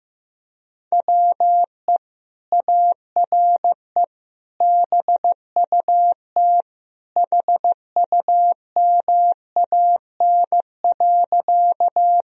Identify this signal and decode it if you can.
Morse code, heard as a keyed tone, about 15 wpm, 700 Hz. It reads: WE ARE BUT HUMAN.
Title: Sentences from Top 1000 Words - Review Part 3 - 15wpm